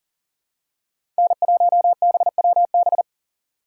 Morse code, heard as a keyed tone, about 40 wpm, 700 Hz. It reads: N1BWB
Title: USA Call Signs - 5 Characters - 4x Character Spacing - 40wpm